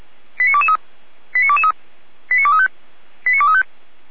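Five-tone selective call (ZVEI), heard as quick runs of five tones, tones of about 70 ms. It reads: 89223 89223 89235 89235
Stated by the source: ZVEI five-tone selective call received by SDR.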